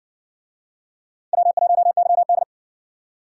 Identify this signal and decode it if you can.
Morse code, heard as a keyed tone, about 60 wpm, 700 Hz. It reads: W1YD